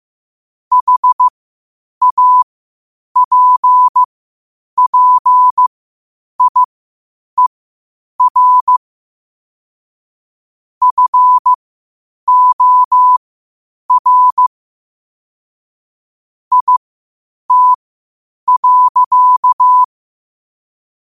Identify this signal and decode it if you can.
Morse code, heard as a keyed tone, about 15 wpm, spaced as for 8 wpm, 1000 Hz. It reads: HAPPIER FOR IT.
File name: qotdfiles / 8wpm